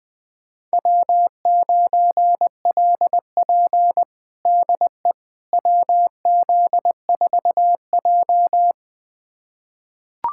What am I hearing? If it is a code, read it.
Morse code, heard as a keyed tone, about 21 wpm, 700 Hz. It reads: W9LP DE WZ4J
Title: USA Callsign DE Callsign - 4 Characters - 20wpm